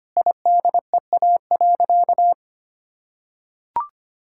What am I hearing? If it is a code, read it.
Morse code, heard as a keyed tone, about 25 wpm, 700 Hz. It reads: IDEA.
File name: practice / 25wpm / Sentences with 4-Word Phrases - 25wpm